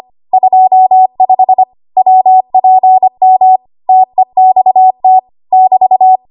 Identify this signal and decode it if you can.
Morse code, heard as a keyed tone, about 25 wpm, 750 Hz. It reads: 25 WPM TEXT <BT>
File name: Archive / 25 WPM / 25WPM